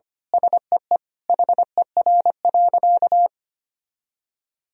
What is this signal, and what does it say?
Morse code, heard as a keyed tone, about 25 wpm, 700 Hz. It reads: SEE HER.
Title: Sentences from Top 1000 Words - Review Part 1 - 25wpm